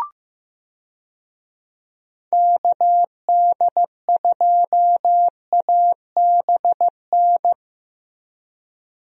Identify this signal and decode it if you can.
Morse code, heard as a keyed tone, about 15 wpm, 700 Hz. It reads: KD2ABN
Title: USA Call Signs - 6 Characters - Repeated 2x - 15wpm